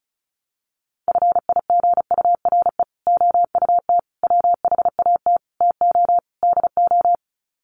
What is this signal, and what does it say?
Morse code, heard as a keyed tone, about 34 wpm, 700 Hz. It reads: FIGURE OUT WHAT TO DO
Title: Sets of 5 Words Encore2 35wpm